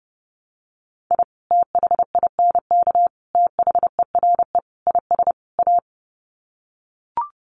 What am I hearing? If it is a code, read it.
Morse code, heard as a keyed tone, about 30 wpm, 700 Hz. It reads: I THINK THERE IS A
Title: Sets of 5 Words 30wpm